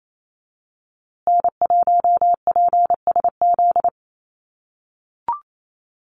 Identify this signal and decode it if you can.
Morse code, heard as a keyed tone, about 28 wpm, 700 Hz. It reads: N1PSZ
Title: USA Call Signs Repeated 3x 28wpm